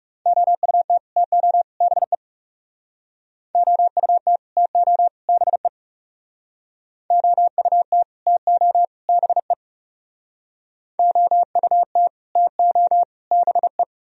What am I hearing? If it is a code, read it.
Morse code, heard as a keyed tone, about 40 wpm, 700 Hz. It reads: OUT TO BE OUT TO BE OUT TO BE OUT TO BE